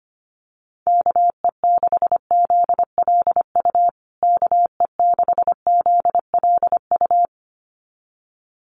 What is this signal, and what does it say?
Morse code, heard as a keyed tone, about 25 wpm, 700 Hz. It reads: KE6ZLU KE6ZLU
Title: USA Call Signs Repeated 2x 25wpm